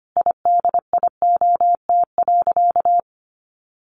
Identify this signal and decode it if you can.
Morse code, heard as a keyed tone, about 25 wpm, 700 Hz. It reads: IDIOT.
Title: A Sound Of Thunder 25wpm